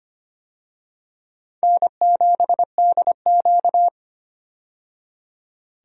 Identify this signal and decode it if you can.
Morse code, heard as a keyed tone, about 25 wpm, 700 Hz. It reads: N7DQ
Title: USA Call Signs - 4 Characters - 25wpm